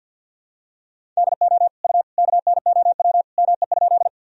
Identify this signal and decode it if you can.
Morse code, heard as a keyed tone, about 45 wpm, 700 Hz. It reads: DO U KNOW ME?
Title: Sentences from Top 200 Words and Abbreviations 50wpm